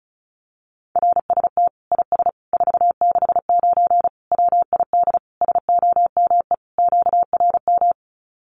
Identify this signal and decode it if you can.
Morse code, heard as a keyed tone, about 34 wpm, 700 Hz. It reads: RST IS 469 WID SOME QRM
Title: Common QSO Phrases 35wpm